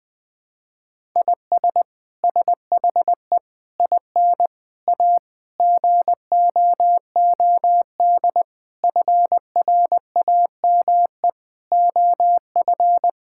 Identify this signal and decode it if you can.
Morse code, heard as a keyed tone, about 21 wpm, 700 Hz. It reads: IS SHE IN A GOOD FRAME OF
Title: Sentences from Top 1000 Words - Review Part 6 - 20wpm